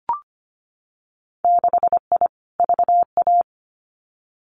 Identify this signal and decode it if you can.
Morse code, heard as a keyed tone, about 25 wpm, 700 Hz. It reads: 6I VA
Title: Winter Field Day Exchanges Encore 25wpm